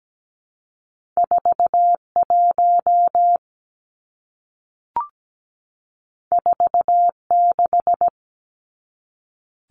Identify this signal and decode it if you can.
Morse code, heard as a keyed tone, about 17 wpm, 700 Hz. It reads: 41 46
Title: Numbers - 2-Digit 17wpm